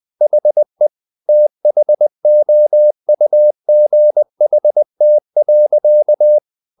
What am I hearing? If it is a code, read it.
Morse code, heard as a keyed tone, about 20 wpm, 600 Hz. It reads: HE THOUGHT.
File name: the old man and the sea part 2 20wpm